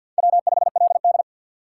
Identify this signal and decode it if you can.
Morse code, heard as a keyed tone, about 51 wpm, 700 Hz. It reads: W5LD
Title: USA Call Signs - 3x Character Spacing - 50wpm